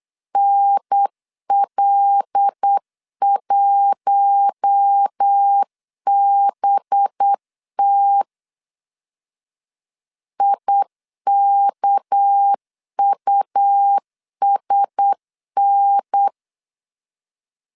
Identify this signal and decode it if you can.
Morse code, heard as a keyed tone, about 8 wpm, 800 Hz. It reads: NL1BT IKUSN
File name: LEZIONI / 6 PKX0 / 6 PK 8wpm 3003